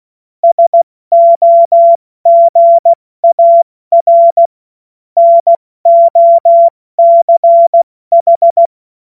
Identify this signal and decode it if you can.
Morse code, heard as a keyed tone, about 16 wpm, 680 Hz. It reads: SOGAR NOCH
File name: Morsekurs / Witze / Witz 02 April 2024 70 F80